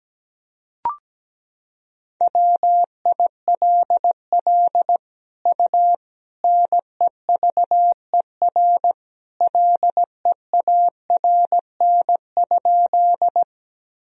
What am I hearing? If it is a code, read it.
Morse code, heard as a keyed tone, about 17 wpm, 700 Hz. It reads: WILL U NEVER LEARN?
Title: Sentences from Top 300 Words and Abbreviations 17wpm